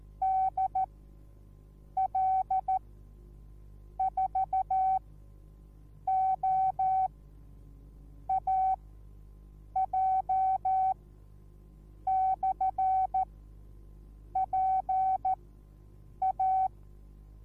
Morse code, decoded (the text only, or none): DL4OAJ/PA